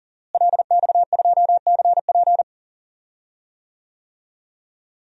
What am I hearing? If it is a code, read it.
Morse code, heard as a keyed tone, about 39 wpm, 700 Hz. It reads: LX2CP
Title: DXCC-100 Call Signs - 5 Characters - Repeated 3x - 40wpm